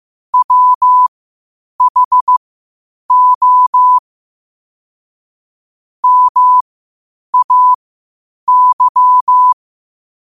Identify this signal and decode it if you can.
Morse code, heard as a keyed tone, about 15 wpm, spaced as for 8 wpm, 1000 Hz. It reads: WHO MAY
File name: qotdfiles / 8wpm